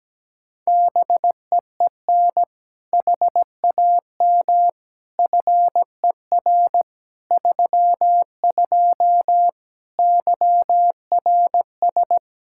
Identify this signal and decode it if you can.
Morse code, heard as a keyed tone, about 17 wpm, 700 Hz. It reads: BEEN HAM FER 32 YRS
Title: K1USN SST QSO Practice 17wpm